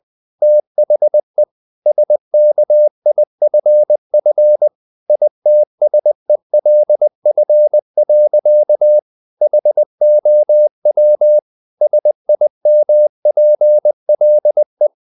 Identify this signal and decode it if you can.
Morse code, heard as a keyed tone, about 20 wpm, 600 Hz. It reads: THE SKIFF ITSELF. HOW SIMPLE
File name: the old man and the sea part 2 20wpm